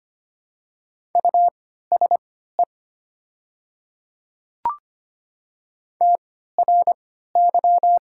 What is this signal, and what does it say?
Morse code, heard as a keyed tone, about 25 wpm, spaced as for 14 wpm, 700 Hz. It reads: USE TRY
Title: 3x Character Spacing - Top 300 Words - 25wpm